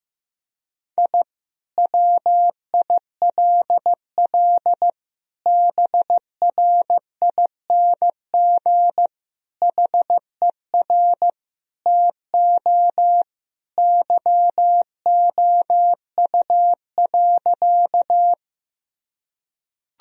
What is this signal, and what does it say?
Morse code, heard as a keyed tone, about 15 wpm, 700 Hz. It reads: I WILL BRING HER TO YOU.
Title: Sentences from Top 1000 Words - Review Part 3 - 15wpm